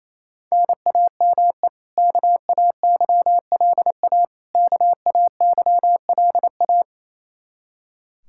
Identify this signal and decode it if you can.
Morse code, heard as a keyed tone, about 28 wpm, 700 Hz. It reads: NAME KAYLA KAYLA
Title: Practice QSO - Exchange 1 - Beginning - 28wpm